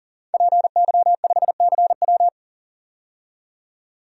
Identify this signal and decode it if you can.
Morse code, heard as a keyed tone, about 40 wpm, 700 Hz. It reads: PY5CW
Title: DXCC-100 Call Signs - 5 Characters - 40wpm